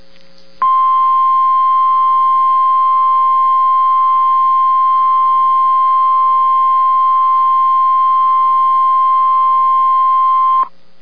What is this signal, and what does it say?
Emergency Alert System (SAME) tones (the 1050 Hz attention tone). It is heard from 0.6 to 10.7 s.